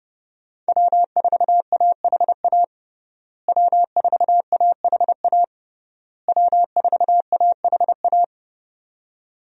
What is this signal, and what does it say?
Morse code, heard as a keyed tone, about 30 wpm, 700 Hz. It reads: W4AHA W4AHA W4AHA
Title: USA Call Signs - 5 Characters - Repeated 3x - 30wpm